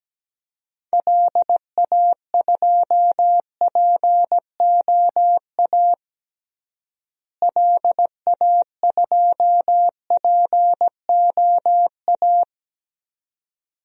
Morse code, heard as a keyed tone, about 17 wpm, 700 Hz. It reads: LA2POA LA2POA
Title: DXCC-100 Call Signs - 6 Characters - Repeated 2x - 17wpm